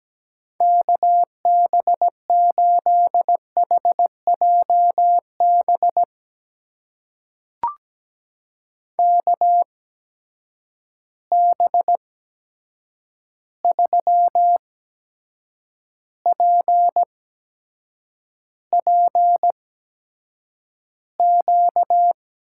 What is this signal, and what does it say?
Morse code, heard as a keyed tone, about 17 wpm, spaced as for 7 wpm, 700 Hz. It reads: KB8HJB KB3PPQ
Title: USA Call Signs - 6 Characters - 8x Character Spacing - 17wpm